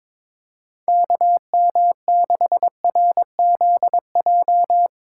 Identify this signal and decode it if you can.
Morse code, heard as a keyed tone, about 22 wpm, 700 Hz. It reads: KM6RZJ
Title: USA Callsign DE Callsign - 6 Characters - 22wpm